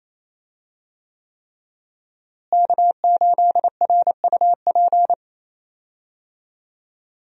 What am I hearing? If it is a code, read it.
Morse code, heard as a keyed tone, about 28 wpm, 700 Hz. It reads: K8RUP
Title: USA Call Signs - 5 Characters - Repeated 3x - 28wpm